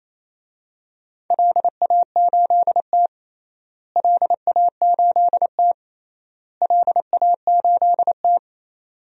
Morse code, heard as a keyed tone, about 28 wpm, 700 Hz. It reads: LA8T LA8T LA8T
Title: DXCC-100 Call Signs - 4 Characters - Repeated 3x - 28wpm